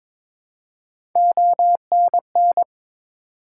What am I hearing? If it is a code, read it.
Morse code, heard as a keyed tone, about 22 wpm, 700 Hz. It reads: ONN
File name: practice / 22wpm / ARRL Sections v2 - 22wpm